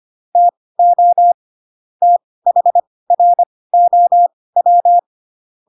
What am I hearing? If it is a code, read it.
Morse code, heard as a keyed tone, about 25 wpm, spaced as for 18 wpm, 700 Hz. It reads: TO THROW
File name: Pinocchio / Pinocchio ch2 12 25 ik1hge